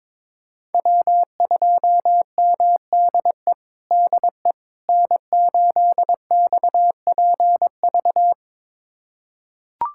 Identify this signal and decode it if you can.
Morse code, heard as a keyed tone, about 22 wpm, 700 Hz. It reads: W2MDE DE N8XPV
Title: USA Callsign DE Callsign - 5 Characters - 22wpm